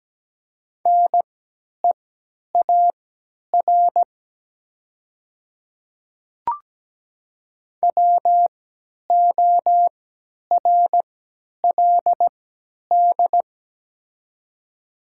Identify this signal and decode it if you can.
Morse code, heard as a keyed tone, about 17 wpm, spaced as for 10 wpm, 700 Hz. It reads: NEAR WORLD
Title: 3x Character Spacing - Top 500 Words - 17wpm